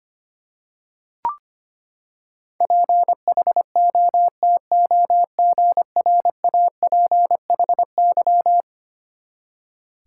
Morse code, heard as a keyed tone, about 25 wpm, 700 Hz. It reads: PHOTOGRAPHY
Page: morse-code-ninja-practice-25wpm